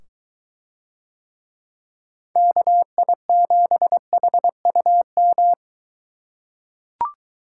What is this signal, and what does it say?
Morse code, heard as a keyed tone, about 21 wpm, 700 Hz. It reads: KI7HUM